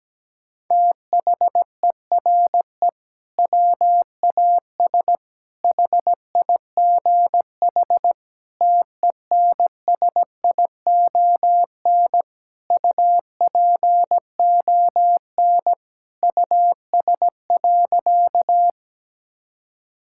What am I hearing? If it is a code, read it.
Morse code, heard as a keyed tone, about 17 wpm, 700 Hz. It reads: THERE WAS HIGH TENSION UPON US.
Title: Sentences from Top 1600 Words 17wpm